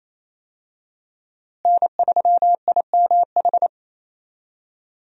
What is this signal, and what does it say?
Morse code, heard as a keyed tone, about 27 wpm, 700 Hz. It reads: N3IMH
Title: USA Call Signs - 5 Characters - Repeated 2x - 28wpm